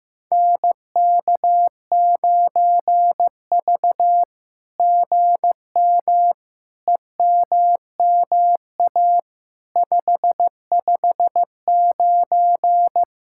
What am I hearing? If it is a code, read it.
Morse code, heard as a keyed tone, about 15 wpm, 700 Hz. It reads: NK9V GM EMMA 559
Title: SOTA QSO Practice 15wpm